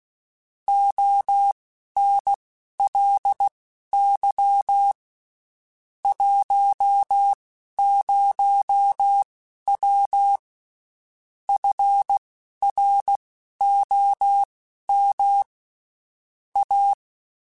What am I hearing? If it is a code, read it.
Morse code, heard as a keyed tone, about 16 wpm, spaced as for 11 wpm, 800 Hz. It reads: ONLY 10W FROM A
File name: (spaced) / QSOs Spaced-mp3s / WA7QOX 2 @15 sp